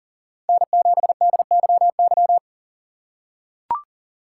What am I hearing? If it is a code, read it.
Morse code, heard as a keyed tone, about 40 wpm, 700 Hz. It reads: N7DYY
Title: USA Call Signs - 5 Characters - 3x Character Spacing - 40wpm